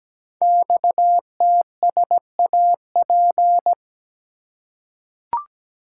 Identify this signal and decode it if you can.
Morse code, heard as a keyed tone, about 17 wpm, 700 Hz. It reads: XTSAP
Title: Code Groups - 5 Letters - 17wpm